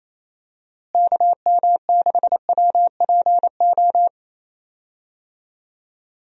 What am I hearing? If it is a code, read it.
Morse code, heard as a keyed tone, about 28 wpm, 700 Hz. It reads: KM6WPO
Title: USA Call Signs - 6 Characters - Repeated 2x - 28wpm